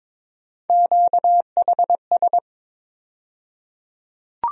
Morse code, heard as a keyed tone, about 22 wpm, 700 Hz. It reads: QHS
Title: USA Call Sign Postfixes - 3 Letters 22wpm